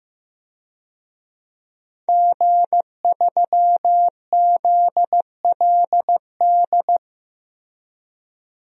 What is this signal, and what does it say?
Morse code, heard as a keyed tone, about 15 wpm, 700 Hz. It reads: G3ZLD
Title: DXCC-100 Call Signs - 5 Characters - Repeated 3x - 15wpm